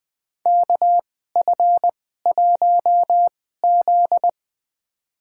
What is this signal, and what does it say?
Morse code, heard as a keyed tone, about 20 wpm, spaced as for 15 wpm, 700 Hz. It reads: KF1Z
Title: USA Call Signs - 4 Characters - 2x Character Spacing - 20wpm